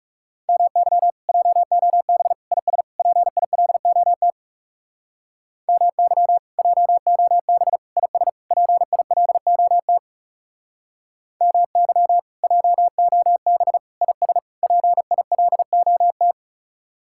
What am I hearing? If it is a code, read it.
Morse code, heard as a keyed tone, about 37 wpm, 700 Hz. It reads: MY JOB IS PILOT MY JOB IS PILOT MY JOB IS PILOT
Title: Speed-Racing Common QSO Phrases Encore - 30wpm